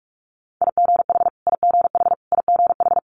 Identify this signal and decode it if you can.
Morse code, heard as a keyed tone, about 46 wpm, 700 Hz. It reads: IGH IGH IGH
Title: Multi-letter Phonemes Repeated 3x 45wpm